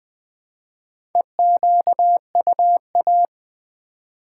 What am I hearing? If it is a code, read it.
Morse code, heard as a keyed tone, about 20 wpm, 700 Hz. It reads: EQUA
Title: Build A Word - 1 Letter At A Time - 5-Letter Words - 20wpm